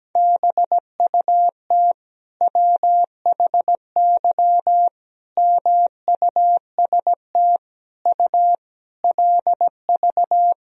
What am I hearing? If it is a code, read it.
Morse code, heard as a keyed tone, about 17 wpm, 700 Hz. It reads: BUT WHY MUST U LV
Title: Sentences from Top 400 Words and Abbreviations 17wpm